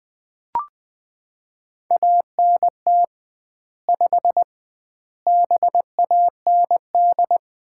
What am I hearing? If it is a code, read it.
Morse code, heard as a keyed tone, about 20 wpm, 700 Hz. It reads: ANT 5 BAND
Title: Common QSO Phrases - 2x Word-Spacing - 20wpm